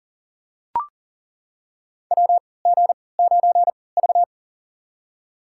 Morse code, heard as a keyed tone, about 40 wpm, spaced as for 23 wpm, 700 Hz. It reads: WG9V
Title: USA Call Signs - 4 Characters - 3x Character Spacing - 40wpm